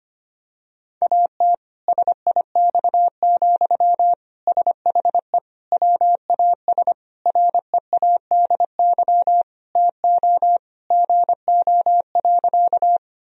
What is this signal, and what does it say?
Morse code, heard as a keyed tone, about 25 wpm, 700 Hz. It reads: AT SIX, SHE WAS READY TO GO.
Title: Sentences from Top 1000 Words - Review Part 2 - 25wpm